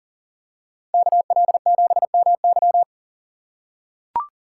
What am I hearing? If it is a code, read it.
Morse code, heard as a keyed tone, about 40 wpm, 700 Hz. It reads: KL7MY
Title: USA Call Signs - 5 Characters - 4x Character Spacing - 40wpm